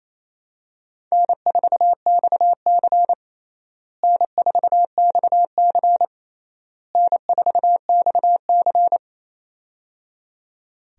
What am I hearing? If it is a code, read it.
Morse code, heard as a keyed tone, about 28 wpm, 700 Hz. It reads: N4XC N4XC N4XC
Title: USA Call Signs - 4 Characters - Repeated 3x - 28wpm